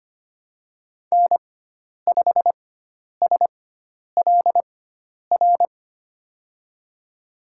Morse code, heard as a keyed tone, about 25 wpm, spaced as for 10 wpm, 700 Hz. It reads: N5SLR
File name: Lesson 16g - Letter c - Call Signs 25wpm